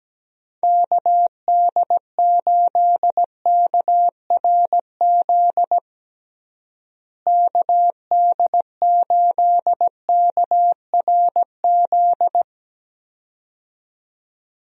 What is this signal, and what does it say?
Morse code, heard as a keyed tone, about 17 wpm, 700 Hz. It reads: KD8KRZ KD8KRZ
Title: USA Call Signs - 6 Characters - Repeated 2x - 17wpm